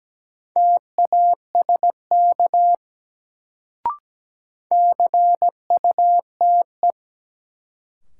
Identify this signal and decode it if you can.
Morse code, heard as a keyed tone, about 17 wpm, 700 Hz. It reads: TASK CUTE
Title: Sentences from Top 1000 Words - Review Part 7 - New Word Drill - 17wpm